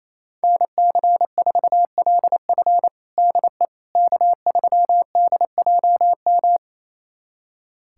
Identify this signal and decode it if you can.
Morse code, heard as a keyed tone, about 28 wpm, 700 Hz. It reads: NC4LF DE K3DJM
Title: USA Callsign DE Callsign - 5 Characters - 28wpm